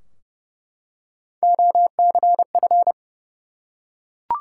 Morse code, heard as a keyed tone, about 30 wpm, 700 Hz. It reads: OCF
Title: Top 180 QSO Elements 30wpm